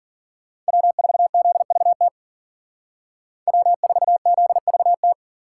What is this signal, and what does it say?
Morse code, heard as a keyed tone, about 50 wpm, 700 Hz. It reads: W4ZVT W4ZVT
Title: Speed Racing - USA 1x3 Call Signs 40wpm